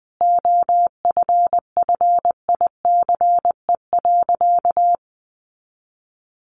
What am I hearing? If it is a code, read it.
Morse code, heard as a keyed tone, about 19 wpm, 700 Hz. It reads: OFFICE.
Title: Sentences from Top 1000 Words 20wpm